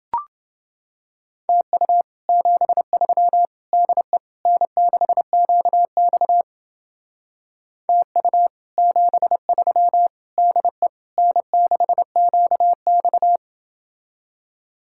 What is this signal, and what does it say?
Morse code, heard as a keyed tone, about 30 wpm, 700 Hz. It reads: TU 73 DE N6QX TU 73 DE N6QX